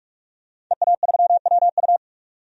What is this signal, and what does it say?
Morse code, heard as a keyed tone, about 46 wpm, 700 Hz. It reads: EA3WU
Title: DXCC-100 Call Signs - 5 Characters - 45wpm